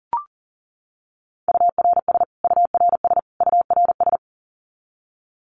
Morse code, heard as a keyed tone, about 40 wpm, 700 Hz. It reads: URS URS URS
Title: Abbreviations Repeated 3x - 40wpm